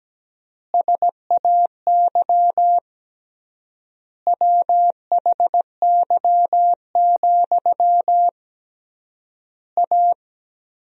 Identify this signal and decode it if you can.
Morse code, heard as a keyed tone, about 17 wpm, 700 Hz. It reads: SAY WHY, A
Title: 3x Word-Spacing - Sentences from Top 500 Words - 17wpm